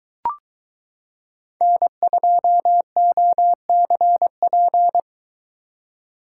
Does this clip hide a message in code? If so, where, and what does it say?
Yes: Morse code from 1.6 to 5.0 s, heard as a keyed tone, about 23 wpm, 700 Hz. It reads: N2OCP